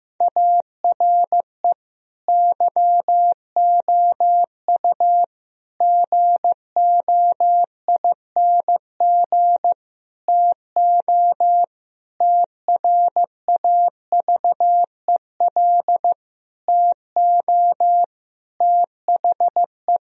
Morse code, heard as a keyed tone, about 15 wpm, 700 Hz. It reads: ARE YOU GOING TO TRAVEL TO THE
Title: Sentences from Top 1000 Words - Review Part 3 - 15wpm